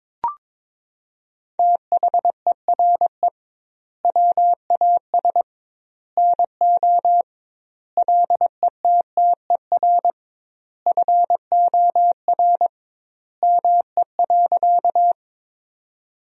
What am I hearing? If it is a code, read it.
Morse code, heard as a keyed tone, about 22 wpm, 700 Hz. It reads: THERE WAS NO LETTER FOR ME.
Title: 2x Word-Spacing - Sentences from Top 300 Words - 22wpm